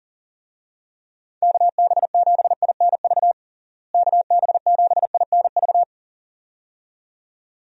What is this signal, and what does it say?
Morse code, heard as a keyed tone, about 37 wpm, 700 Hz. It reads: KB7INV KB7INV
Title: USA Call Signs - 6 Characters - Repeated 2x - 40wpm